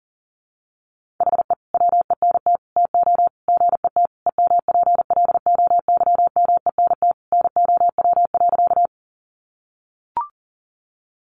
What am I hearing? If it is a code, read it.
Morse code, heard as a keyed tone, about 40 wpm, 700 Hz. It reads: HE WENT TO GET EMPLOYMENT NOW.
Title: Sentences from Top 1500 Words 40wpm